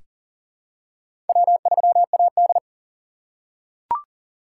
Morse code, heard as a keyed tone, about 40 wpm, 700 Hz. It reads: W3AD
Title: USA Call Signs - 2x Character Spacing - 40wpm